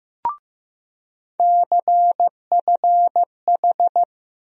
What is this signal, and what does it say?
Morse code, heard as a keyed tone, about 15 wpm, 700 Hz. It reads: CFH
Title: USA Call Sign Postfixes - 3 Letters 15wpm